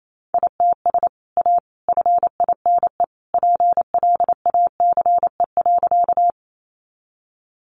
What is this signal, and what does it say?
Morse code, heard as a keyed tone, about 28 wpm, 700 Hz. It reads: ITS A FINE PLACE.
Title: Sentences from Top 600 Words 28wpm